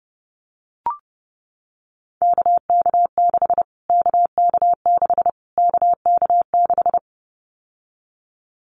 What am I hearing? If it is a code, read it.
Morse code, heard as a keyed tone, about 30 wpm, 700 Hz. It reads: KK6 KK6 KK6
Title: Callsign Prefixes for Top 25 DXCC - Repeated 3x - 30wpm